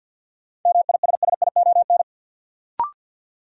Morse code, heard as a keyed tone, about 53 wpm, 700 Hz. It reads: MISSION
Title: Sentences from Top 1600 Words - New Word Drill - 50wpm